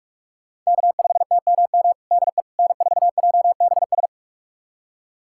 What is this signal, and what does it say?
Morse code, heard as a keyed tone, about 46 wpm, 700 Hz. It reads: K5TMM DE N4JBS
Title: USA Callsign DE Callsign - 5 Characters - 45wpm